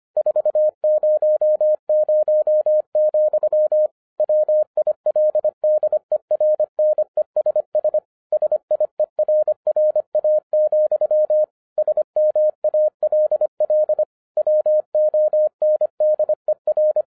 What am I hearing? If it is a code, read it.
Morse code, heard as a keyed tone, about 26 wpm, 610 Hz. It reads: 400, WILDERNESS SIERRA, SMALL WONDER
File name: 2011-07-mp3 / n24 QRP OTAKU 25wpm